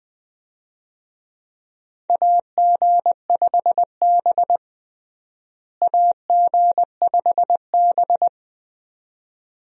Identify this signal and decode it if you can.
Morse code, heard as a keyed tone, about 20 wpm, 700 Hz. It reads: AG5B AG5B
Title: USA Call Signs - 4 Characters - Repeated 2x - 20wpm